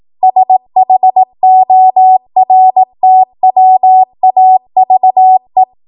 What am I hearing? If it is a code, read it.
Morse code, heard as a keyed tone, about 18 wpm, 750 Hz. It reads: SHORTWAVE
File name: Morse / 18WPM